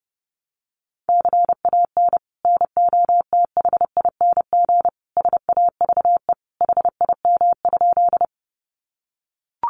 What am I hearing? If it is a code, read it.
Morse code, heard as a keyed tone, about 30 wpm, 700 Hz. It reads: CAN NOTHING SAVE HIM?
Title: Sentences from Top 1100 Words 30wpm